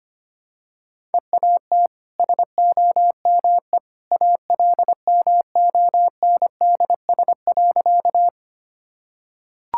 Morse code, heard as a keyed tone, about 25 wpm, 690 Hz. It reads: EAT SOME ALMONDS.
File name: Easy Sentences with Hints - 25wpm